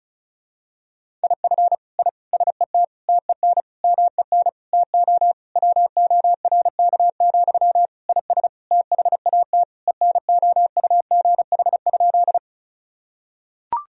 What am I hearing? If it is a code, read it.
Morse code, heard as a keyed tone, about 35 wpm, 700 Hz. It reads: IF I SET TEN MEN TO WORK, IS THAT ENOUGH?
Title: Sentences from Top 1000 Words - Review Part 2 - 35wpm